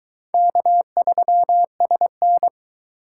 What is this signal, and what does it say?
Morse code, heard as a keyed tone, about 21 wpm, 700 Hz. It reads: K3SN